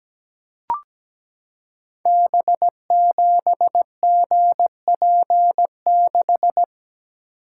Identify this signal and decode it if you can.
Morse code, heard as a keyed tone, about 17 wpm, 700 Hz. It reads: B7GP6
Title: Code Groups - 5 Letter:Numbers - 17wpm